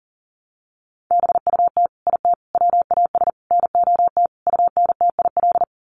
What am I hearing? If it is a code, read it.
Morse code, heard as a keyed tone, about 40 wpm, 700 Hz. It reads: BUT IT WAS NOT UNTIL
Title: Sets of 5 Words Encore2 40wpm